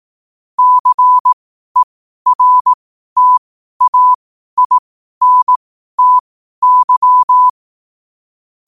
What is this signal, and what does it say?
Morse code, heard as a keyed tone, about 18 wpm, spaced as for 12 wpm, 1000 Hz. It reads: CERTAINTY
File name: qotdfiles / 12wpm